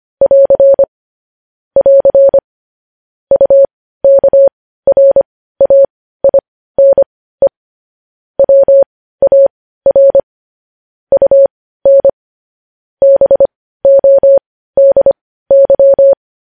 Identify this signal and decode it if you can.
Morse code, heard as a keyed tone, about 25 wpm, spaced as for 15 wpm, 550 Hz. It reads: <AR> <AR> UKRAINE WAR UN BODY
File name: podcast / News headlines-1x-25x15